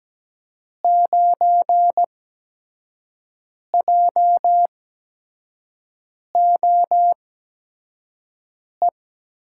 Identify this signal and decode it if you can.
Morse code, heard as a keyed tone, about 17 wpm, 700 Hz. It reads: 9 J O E
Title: USA Call Signs - 8x Character Spacing - 17wpm